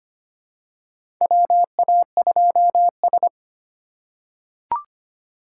Morse code, heard as a keyed tone, about 25 wpm, 700 Hz. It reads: WA2S
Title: USA Call Signs - 4 Characters - 2x Character Spacing - 25wpm